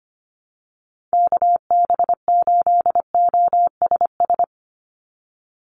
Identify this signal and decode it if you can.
Morse code, heard as a keyed tone, about 25 wpm, 700 Hz. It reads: KB8OSS
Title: USA Call Signs 25wpm